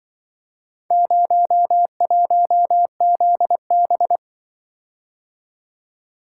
Morse code, heard as a keyed tone, about 24 wpm, 700 Hz. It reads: 01ZB
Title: Speed-Racing - Four Letter-Number Combinations - 20wpm